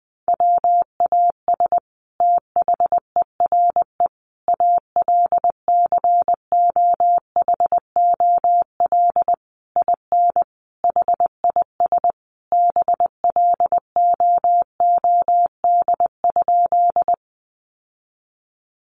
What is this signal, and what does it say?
Morse code, heard as a keyed tone, about 20 wpm, 700 Hz. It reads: WAS THERE ALCOHOL IN HIS BLOOD?
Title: Sentences from Top 1400 Words 20wpm